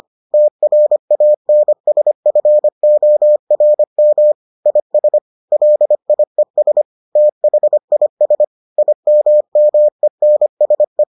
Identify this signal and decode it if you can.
Morse code, heard as a keyed tone, about 25 wpm, 600 Hz. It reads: TRANSFORM IS LIES THIS IMMENSE